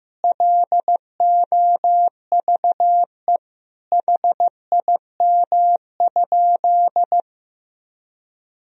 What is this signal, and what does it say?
Morse code, heard as a keyed tone, about 15 wpm, 700 Hz. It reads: LOVE HIM?
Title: Sentences from Top 400 Words and Abbreviations 15wpm